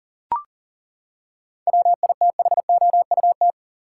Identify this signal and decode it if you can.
Morse code, heard as a keyed tone, about 40 wpm, 700 Hz. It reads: WITHOUT
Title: Seven Letter Words - 40wpm